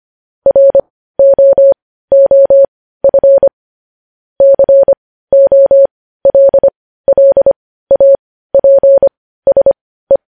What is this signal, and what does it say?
Morse code, heard as a keyed tone, about 25 wpm, spaced as for 15 wpm, 550 Hz. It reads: ROOF COLLAPSE
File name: podcast / News headlines-1x-25x15